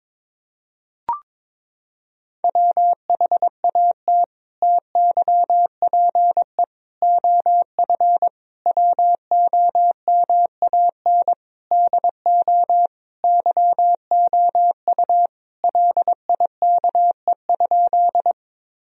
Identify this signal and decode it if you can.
Morse code, heard as a keyed tone, about 22 wpm, 700 Hz. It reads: WHAT TYPE OF WOMAN DO YOU LIKE?
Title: Sentences from Top 1000 Words - Review Part 3 - 22wpm